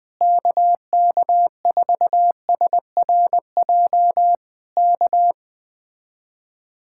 Morse code, heard as a keyed tone, about 20 wpm, 700 Hz. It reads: KK4SRJ K
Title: Speed-Racing Common QSO Phrases - 17wpm